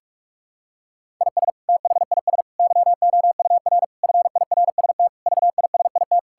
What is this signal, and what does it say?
Morse code, heard as a keyed tone, about 44 wpm, 700 Hz. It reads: IS THIS YOUR FIRST VISIT
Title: Sentences from Top 1000 Words - Review Part 6 - 45wpm